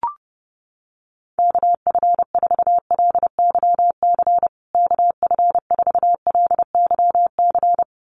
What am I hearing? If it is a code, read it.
Morse code, heard as a keyed tone, about 30 wpm, 700 Hz. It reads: KF4LYC KF4LYC
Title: USA Call Signs Repeated 2x 30wpm